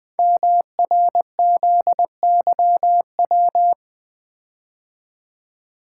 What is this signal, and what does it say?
Morse code, heard as a keyed tone, about 20 wpm, 700 Hz. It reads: MRZYW
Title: Code Groups - 5 Letters - 20wpm